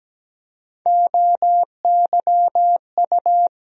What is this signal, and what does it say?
Morse code, heard as a keyed tone, about 17 wpm, 700 Hz. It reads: OYU
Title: USA Call Sign Postfixes - 3 Letters 17wpm